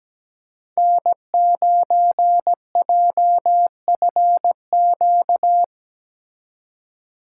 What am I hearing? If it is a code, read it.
Morse code, heard as a keyed tone, about 19 wpm, 700 Hz. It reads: N9JFQ
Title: Speed Racing - USA 1x3 Call Signs 17wpm